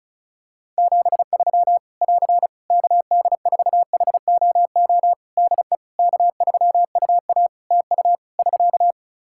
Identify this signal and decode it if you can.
Morse code, heard as a keyed tone, about 36 wpm, 700 Hz. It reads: 73 <AR> KD4HOO DE K3UA TU <SK>
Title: Practice QSO - Exchange 4 - Ending - 35wpm